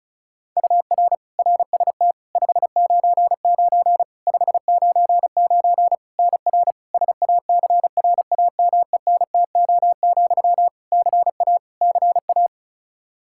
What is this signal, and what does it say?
Morse code, heard as a keyed tone, about 36 wpm, 700 Hz. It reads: UR RST 599 599 NR SACRAMENTO, CA CA
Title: K1USN SST QSO Practice 35wpm